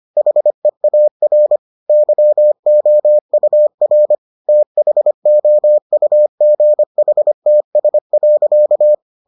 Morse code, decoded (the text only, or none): HEAR YOUR THOUGHTS.